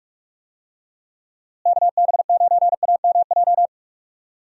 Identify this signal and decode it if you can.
Morse code, heard as a keyed tone, about 43 wpm, 700 Hz. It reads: KB9AMJ
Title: USA Call Signs - 6 Characters - Repeated 3x - 45wpm